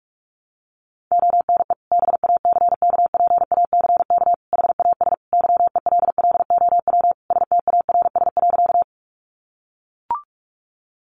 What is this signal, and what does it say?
Morse code, heard as a keyed tone, about 44 wpm, 700 Hz. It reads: ONE BACKPACK HAS YELLOW STARS.
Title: Additional Easy Phrases 45wpm